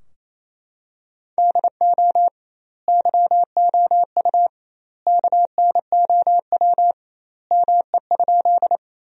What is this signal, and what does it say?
Morse code, heard as a keyed tone, about 27 wpm, 700 Hz. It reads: DO YOU KNOW ME?
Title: 2x Word-Spacing - Sentences from Top 200 Words - 28wpm